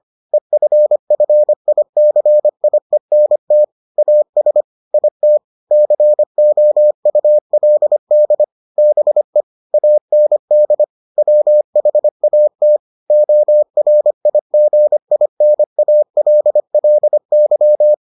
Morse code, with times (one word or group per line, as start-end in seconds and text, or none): 0.3-3.6 EFFICIENT
4.0-4.6 AS
4.9-5.4 IT
5.7-8.4 COULD
8.8-9.4 BE
9.7-10.8 AND
11.2-12.8 WHAT
13.1-17.9 ORIGINALLY